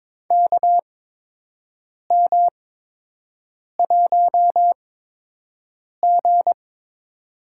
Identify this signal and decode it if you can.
Morse code, heard as a keyed tone, about 22 wpm, spaced as for 6 wpm, 700 Hz. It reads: KM1G
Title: USA Call Signs - 4 Characters - 8x Character Spacing - 22wpm